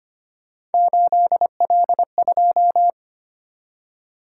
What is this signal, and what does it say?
Morse code, heard as a keyed tone, about 25 wpm, 700 Hz. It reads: 8L2
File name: Three Letter-Number Combinations - 25wpm